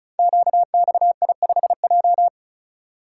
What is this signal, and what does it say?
Morse code, heard as a keyed tone, about 35 wpm, 700 Hz. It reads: QXI5J
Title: Code Groups - 5 Letter:Numbers - 35wpm